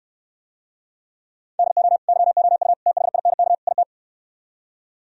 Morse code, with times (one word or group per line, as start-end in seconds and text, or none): 1.6-2.0 DO
2.1-2.7 YOU
2.9-3.5 THINK
3.7-3.8 IT